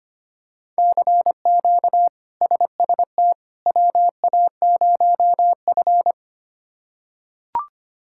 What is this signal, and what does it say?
Morse code, heard as a keyed tone, about 25 wpm, 700 Hz. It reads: CQ SST WA0F
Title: K1USN SST QSO Practice - 25wpm